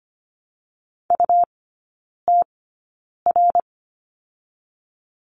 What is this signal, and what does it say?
Morse code, heard as a keyed tone, about 25 wpm, spaced as for 9 wpm, 700 Hz. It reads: UTR
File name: Lesson 17e - Letter u - Three Characters 25wpm